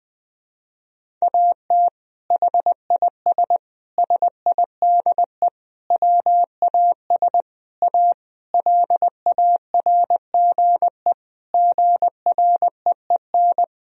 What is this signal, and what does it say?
Morse code, heard as a keyed tone, about 20 wpm, 700 Hz. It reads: AT HIS SIDE WAS A LARGE GREEN
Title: Sentences from Top 1000 Words - Review Part 2 - 20wpm